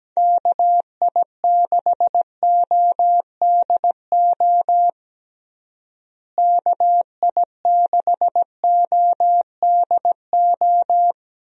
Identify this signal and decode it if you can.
Morse code, heard as a keyed tone, about 17 wpm, 700 Hz. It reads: KI6ODO KI6ODO
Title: USA Call Signs - 6 Characters - Repeated 3x - 17wpm